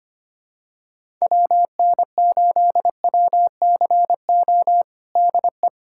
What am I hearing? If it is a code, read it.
Morse code, heard as a keyed tone, about 25 wpm, 700 Hz. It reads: WN8WCO DE
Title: USA Callsign DE Callsign - 6 Characters - 25wpm